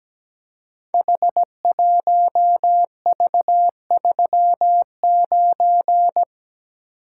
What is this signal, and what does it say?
Morse code, heard as a keyed tone, about 17 wpm, 700 Hz. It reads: H1V39
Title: Code Groups - 5 Letter:Numbers - 17wpm